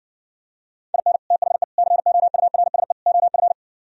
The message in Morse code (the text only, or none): IN THE COURSE OF